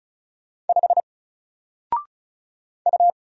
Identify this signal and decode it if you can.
Morse code, heard as a keyed tone, about 34 wpm, 700 Hz. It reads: 5 U
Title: Single Letter-Number 35wpm